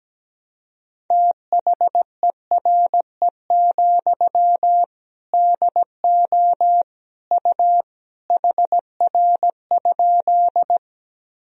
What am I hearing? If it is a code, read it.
Morse code, heard as a keyed tone, about 17 wpm, 700 Hz. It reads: THERE, DO U HR?
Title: Sentences from Top 300 Words and Abbreviations 17wpm